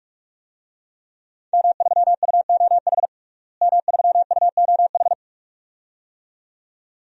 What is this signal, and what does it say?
Morse code, heard as a keyed tone, about 46 wpm, 700 Hz. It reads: M3UOH M3UOH
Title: DXCC-100 Call Signs - 5 Characters - Repeated 2x - 45wpm